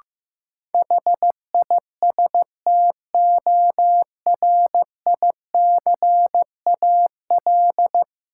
Morse code, heard as a keyed tone, about 15 wpm, 700 Hz. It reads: HISTORICAL
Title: Sentences from Top 600 Words - New Word Drill - 15wpm